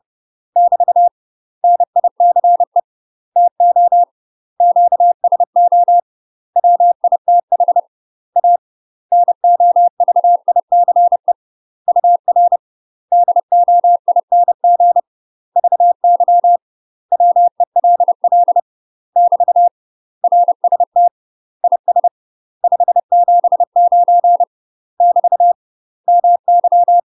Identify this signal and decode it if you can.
Morse code, heard as a keyed tone, about 30 wpm, 700 Hz. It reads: <BT> NICE TO QSO WITH A NOVICE UR DOING VY WELL <BT> RST IS 579 <BT> MY
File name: UK MASTER QSOs TRAINING / 30WPM / UK MASTER QSOs 0048